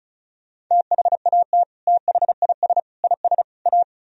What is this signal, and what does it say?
Morse code, heard as a keyed tone, about 41 wpm, 700 Hz. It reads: THAT THIS IS A
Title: Speed-Racing - Sets of 4 Words 35wpm